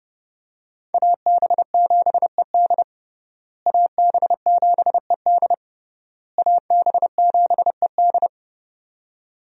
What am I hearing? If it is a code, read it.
Morse code, heard as a keyed tone, about 30 wpm, 700 Hz. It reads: AB7ED AB7ED AB7ED
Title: USA Call Signs - 5 Characters - Repeated 3x - 30wpm